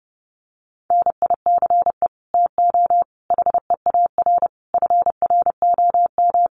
Morse code, heard as a keyed tone, about 30 wpm, 700 Hz. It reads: NICE TO HEAR FROM
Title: Common QSO Phrases 30wpm